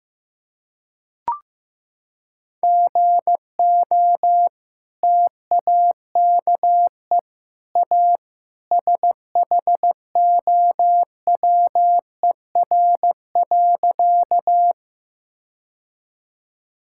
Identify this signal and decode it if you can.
Morse code, heard as a keyed tone, about 16 wpm, 700 Hz. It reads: GO TAKE A SHOWER.
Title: Sentences from Top 1600 Words 15wpm